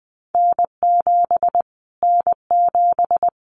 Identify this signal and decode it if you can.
Morse code, heard as a keyed tone, about 20 wpm, 700 Hz. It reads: N7 N7
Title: Callsign Prefixes for Top 25 DXCC - Repeated 3x - 20wpm